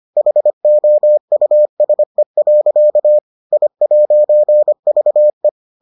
Morse code, heard as a keyed tone, about 25 wpm, 600 Hz. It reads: HOUSE. I'VE